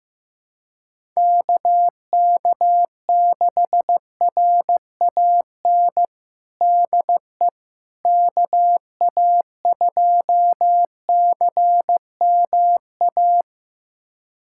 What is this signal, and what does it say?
Morse code, heard as a keyed tone, about 15 wpm, 700 Hz. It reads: KK6RAN DE KA2CMA
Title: USA Callsign DE Callsign - 6 Characters - 15wpm